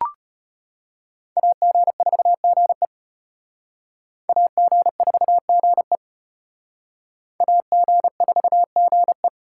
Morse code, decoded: AG4GE AG4GE AG4GE

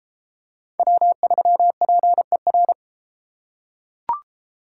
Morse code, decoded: W3PER

